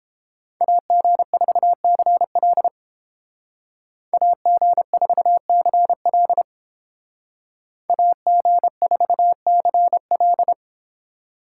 Morse code, heard as a keyed tone, about 32 wpm, 700 Hz. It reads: AG4CL AG4CL AG4CL